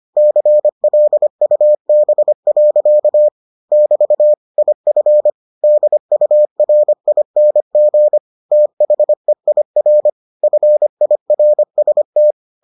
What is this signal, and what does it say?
Morse code, heard as a keyed tone, about 25 wpm, 600 Hz. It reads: CLUB. <BT> IF DURING THEIR FIRST